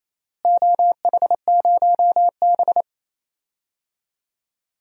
Morse code, heard as a keyed tone, about 28 wpm, 700 Hz. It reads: OH0B
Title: DXCC-100 Call Signs - 4 Characters - Repeated 3x - 28wpm